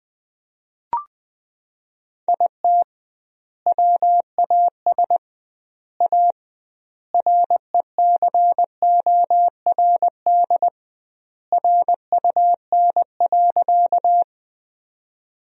Morse code, heard as a keyed tone, about 20 wpm, 700 Hz. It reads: IT WAS A RECORD RUN.
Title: 2x Word-Spacing - Sentences from Top 500 Words - 20wpm